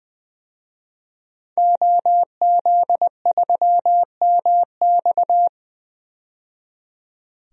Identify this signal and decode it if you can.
Morse code, heard as a keyed tone, about 20 wpm, 700 Hz. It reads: OZ3MX
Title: DXCC-100 Call Signs - 5 Characters - 20wpm